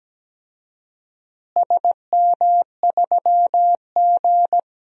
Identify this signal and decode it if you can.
Morse code, heard as a keyed tone, about 17 wpm, 700 Hz. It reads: SM3G
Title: DXCC-100 Call Signs - 4 Characters - Repeated 2x - 17wpm